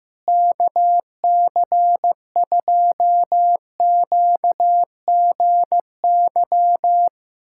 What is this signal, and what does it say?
Morse code, heard as a keyed tone, about 15 wpm, 700 Hz. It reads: KC2QGY
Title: USA Callsign DE Callsign - 6 Characters - 15wpm